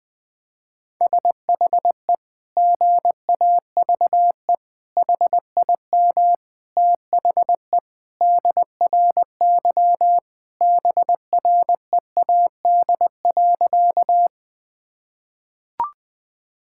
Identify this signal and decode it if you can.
Morse code, heard as a keyed tone, about 20 wpm, 700 Hz. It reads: SHE GAVE HIM THE DRY BREAD.
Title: Sentences from Top 1600 Words 20wpm